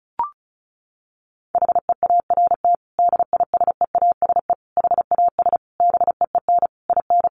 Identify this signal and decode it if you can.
Morse code, heard as a keyed tone, about 34 wpm, 700 Hz. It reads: HEART DISEASE HAS BEEN IN
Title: Sentences from Top 1200 Words 35wpm